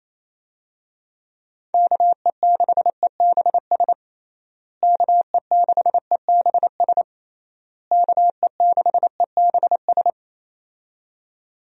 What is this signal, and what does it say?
Morse code, heard as a keyed tone, about 28 wpm, 700 Hz. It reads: KE6EBS KE6EBS KE6EBS
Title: USA Call Signs - 6 Characters - Repeated 3x - 28wpm